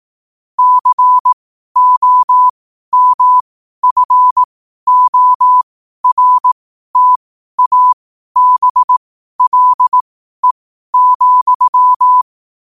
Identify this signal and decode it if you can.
Morse code, heard as a keyed tone, about 18 wpm, spaced as for 12 wpm, 1000 Hz. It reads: COMFORTABLE,
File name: qotdfiles / 12wpm